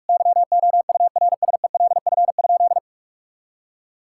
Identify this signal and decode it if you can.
Morse code, heard as a keyed tone, about 46 wpm, 700 Hz. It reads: YOURSELF?
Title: Sentences from Top 1600 Words 45wpm